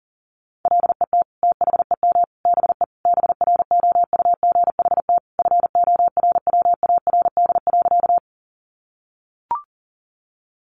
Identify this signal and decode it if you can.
Morse code, heard as a keyed tone, about 40 wpm, 700 Hz. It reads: LET THEM BE BROUGHT FORWARD.
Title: Sentences from Top 1300 Words 40wpm